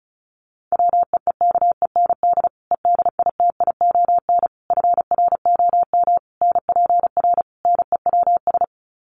Morse code, heard as a keyed tone, about 34 wpm, 700 Hz. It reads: WEEKEND EDITION FROM NPR NEWS
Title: Sets of 5 Words 35wpm